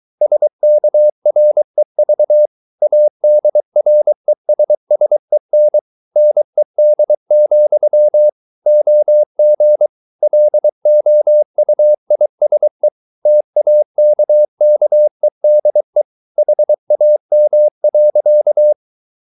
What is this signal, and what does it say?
Morse code, heard as a keyed tone, about 23 wpm, 600 Hz. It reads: SKREV ADRESSEN NED, OG LOUISE TAKKEDE HAM.
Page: Ferie 23 wpm